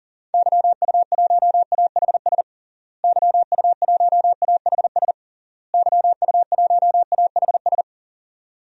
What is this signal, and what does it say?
Morse code, heard as a keyed tone, about 40 wpm, 700 Hz. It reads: YU1AHS YU1AHS YU1AHS
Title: DXCC-100 Call Signs - 6 Characters - Repeated 3x - 40wpm